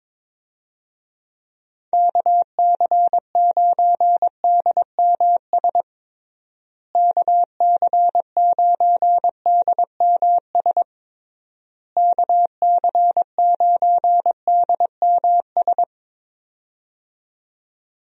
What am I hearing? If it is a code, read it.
Morse code, heard as a keyed tone, about 22 wpm, 700 Hz. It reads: KC9DMS KC9DMS KC9DMS
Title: USA Call Signs - 6 Characters - Repeated 3x - 22wpm